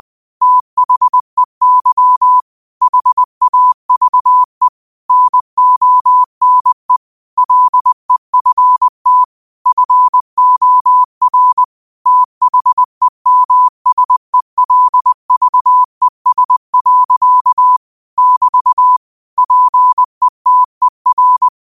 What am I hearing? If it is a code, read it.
Morse code, heard as a keyed tone, about 20 wpm, 1000 Hz. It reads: THEY HAVE NONE LEFT FOR THEMSELVES. <BT> PETER
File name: qotdfiles / 20wpm